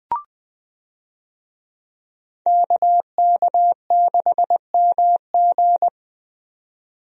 Morse code, heard as a keyed tone, about 20 wpm, 700 Hz. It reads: KK6MG
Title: USA Call Signs - 5 Characters - 20wpm